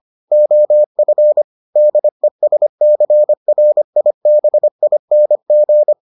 Morse code, heard as a keyed tone, about 25 wpm, 600 Hz. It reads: OF DESCRIBING